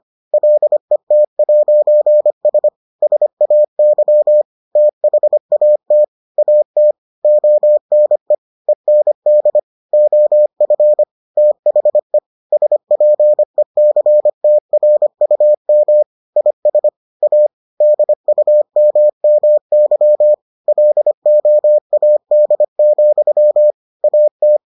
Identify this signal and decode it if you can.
Morse code, heard as a keyed tone, about 25 wpm, 600 Hz. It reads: LET'S SAY THAT AT ONE END OF THE SPECTRUM IS A DUMMY LOAD, AT